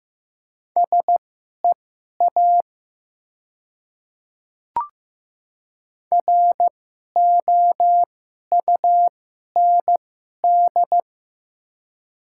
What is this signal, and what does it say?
Morse code, heard as a keyed tone, about 15 wpm, spaced as for 11 wpm, 700 Hz. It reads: SEA ROUND
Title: 2x Character Spacing - Top 300 Words - 15wpm